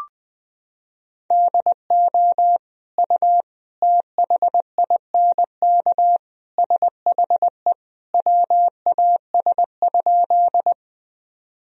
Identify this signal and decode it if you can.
Morse code, heard as a keyed tone, about 21 wpm, 700 Hz. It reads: DO U THINK SHE WAS?
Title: Sentences from Top 200 Words and Abbreviations 20wpm